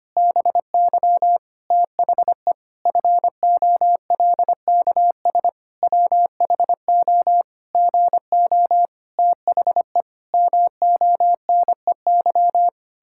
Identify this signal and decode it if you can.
Morse code, heard as a keyed tone, about 25 wpm, 700 Hz. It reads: BY THE FOLKS WHO GO THE MONEY
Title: Pangram Challenge v2 - 25wpm